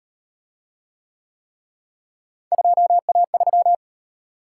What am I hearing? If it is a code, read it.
Morse code, heard as a keyed tone, about 38 wpm, 700 Hz. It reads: 2A3